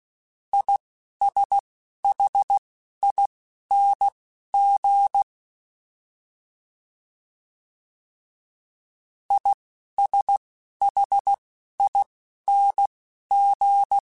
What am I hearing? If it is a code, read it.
Morse code, heard as a keyed tone, about 16 wpm, spaced as for 12 wpm, 800 Hz. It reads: ISHING ISHING
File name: Combinations / Combinations-mp3s / Combinations@10